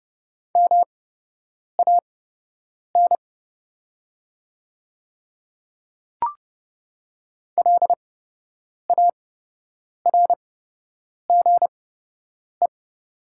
Morse code, heard as a keyed tone, about 30 wpm, spaced as for 8 wpm, 700 Hz. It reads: MAN LARGE
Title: 8x Character Spacing - Top 300 Words - 30wpm